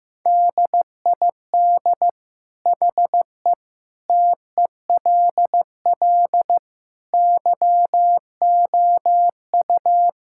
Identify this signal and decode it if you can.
Morse code, heard as a keyed tone, about 15 wpm, 700 Hz. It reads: DID HE TELL YOU
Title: Sentences from Top 1600 Words 15wpm